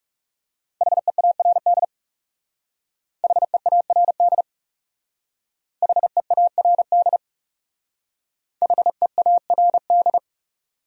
Morse code, heard as a keyed tone, about 36 wpm, 700 Hz. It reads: HEARD HEARD HEARD HEARD
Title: Speed-Racing Top 500 Words 30wpm